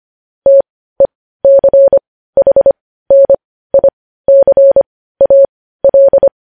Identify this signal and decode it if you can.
Morse code, heard as a keyed tone, about 25 wpm, spaced as for 15 wpm, 550 Hz. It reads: TECHNICAL
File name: podcast / News headlines-1x-25x15